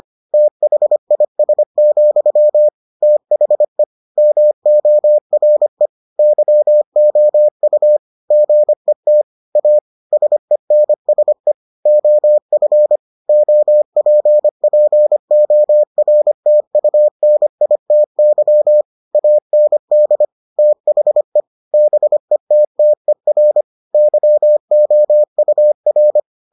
Morse code, heard as a keyed tone, about 25 wpm, 600 Hz. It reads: THIS, THE MORE YOU GET A SENSE OF OPPORTUNITY AND THE BETTER YOUR